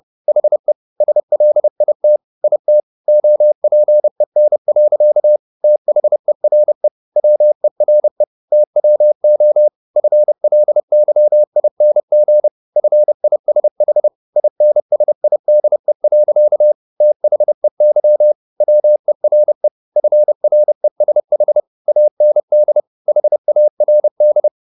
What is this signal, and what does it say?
Morse code, heard as a keyed tone, about 30 wpm, 600 Hz. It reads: HE SLIT IT OPEN. THERE WERE TWO FLYING FISH INSIDE. THEY WERE FRESH AND HARD